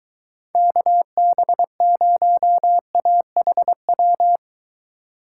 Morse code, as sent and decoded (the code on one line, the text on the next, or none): -.- -... ----- .- .... .--
KB0AHW